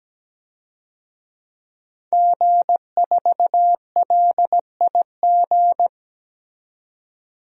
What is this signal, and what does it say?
Morse code, heard as a keyed tone, about 17 wpm, 700 Hz. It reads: G4LIG
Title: DXCC-100 Call Signs - 5 Characters - 17wpm